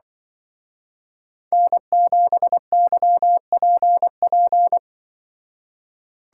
Morse code, heard as a keyed tone, about 25 wpm, 700 Hz. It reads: N7YPP
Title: Speed Racing - USA 1x3 Call Signs 20wpm